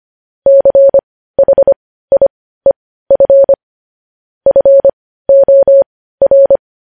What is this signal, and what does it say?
Morse code, heard as a keyed tone, about 25 wpm, spaced as for 15 wpm, 550 Hz. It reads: CHIEF FOR
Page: news-headlines-in-morse-code-at-15-wpm